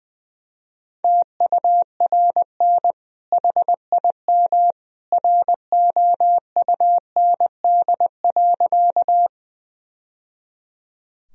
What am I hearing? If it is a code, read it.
Morse code, heard as a keyed tone, about 20 wpm, 700 Hz. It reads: TURN HIM ROUND.
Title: Sentences from Top 1000 Words - Review Part 1 - 20wpm